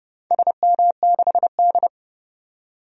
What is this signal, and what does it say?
Morse code, heard as a keyed tone, about 31 wpm, 700 Hz. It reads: SM6D
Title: DXCC-100 Call Signs - 4 Characters - 30wpm